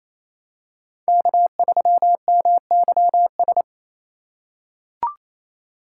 Morse code, heard as a keyed tone, about 27 wpm, 700 Hz. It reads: K3MYS
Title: USA Call Signs - 5 Characters - 2x Character Spacing - 28wpm